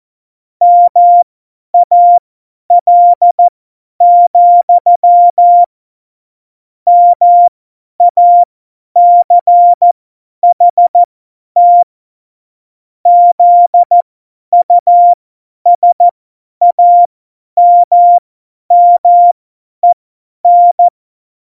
Morse code, heard as a keyed tone, about 14 wpm, spaced as for 10 wpm, 700 Hz. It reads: MAL, MACHT ZUSAMMEN
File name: Morsekurs / Witze / Witz 11 August 2020, 50 F70